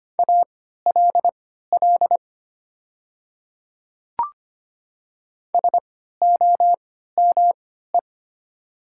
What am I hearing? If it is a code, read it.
Morse code, heard as a keyed tone, about 25 wpm, spaced as for 14 wpm, 700 Hz. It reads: ALL SOME
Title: 3x Character Spacing - Top 100 Words - 25wpm